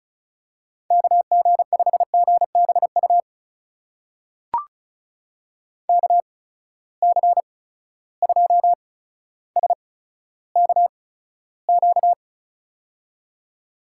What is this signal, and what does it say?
Morse code, heard as a keyed tone, about 34 wpm, spaced as for 15 wpm, 700 Hz. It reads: KG5GBU KC2SKQ
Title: USA Call Signs - 6 Characters - 8x Character Spacing - 35wpm